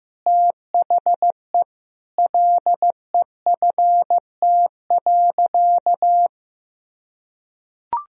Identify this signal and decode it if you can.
Morse code, heard as a keyed tone, about 15 wpm, 700 Hz. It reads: THE LEFT.
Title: Sentences from Top 1000 Words - Review Part 2 - 15wpm